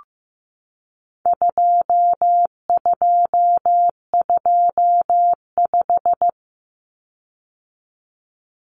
Morse code, heard as a keyed tone, about 15 wpm, 700 Hz. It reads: 2225
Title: Common Cut Numbers - 4-Digit 15wpm